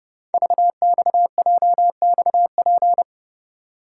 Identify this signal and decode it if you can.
Morse code, heard as a keyed tone, about 30 wpm, 700 Hz. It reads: VXJXP
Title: Code Groups - 5 Letter:Numbers - 30wpm